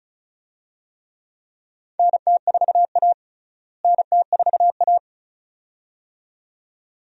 Morse code, heard as a keyed tone, about 36 wpm, 700 Hz. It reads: NT4A NT4A
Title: USA Call Signs - 4 Characters - Repeated 2x - 35wpm